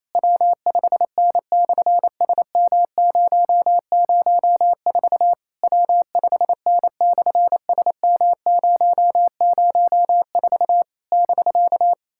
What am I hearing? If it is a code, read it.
Morse code, heard as a keyed tone, about 27 wpm, 700 Hz. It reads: W5N/SM004 W5N/SM004 <BK>
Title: SOTA QSO Practice 28wpm